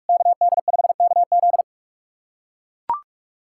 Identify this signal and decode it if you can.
Morse code, heard as a keyed tone, about 44 wpm, 690 Hz. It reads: KD5KZ
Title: SOTA QSO Practice 45wpm